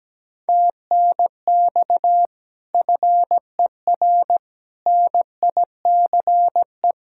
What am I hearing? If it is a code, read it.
Morse code, heard as a keyed tone, about 17 wpm, 700 Hz. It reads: TNX FER NICE
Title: Speed-Racing Common QSO Phrases - 17wpm